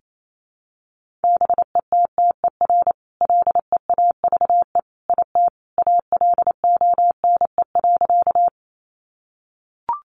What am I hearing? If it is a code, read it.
Morse code, heard as a keyed tone, about 28 wpm, 700 Hz. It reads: BETTER LEAVE IT ALONE.
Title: Sentences from Top 1000 Words 28wpm